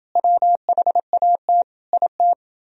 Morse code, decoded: WHAT IT

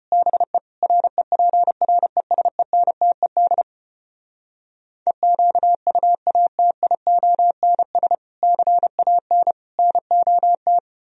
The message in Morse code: BE REPRESENTED EQUATIONS CAN NOT

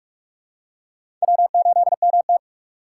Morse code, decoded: W8MT

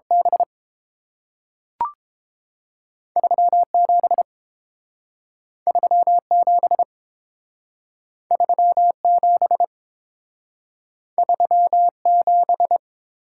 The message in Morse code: B 37 37 37 37

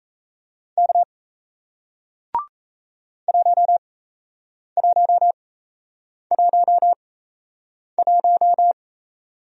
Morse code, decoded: K 1111

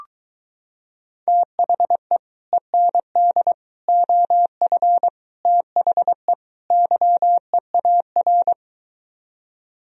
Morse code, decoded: THE END OF THE YEAR

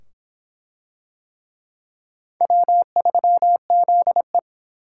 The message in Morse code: W3ZE

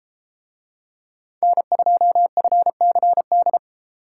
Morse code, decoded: N2FCD

